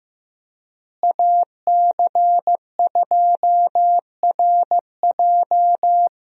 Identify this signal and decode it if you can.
Morse code, heard as a keyed tone, about 16 wpm, 700 Hz. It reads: AC2RJ
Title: USA Call Signs - 5 Characters - Repeated 3x - 15wpm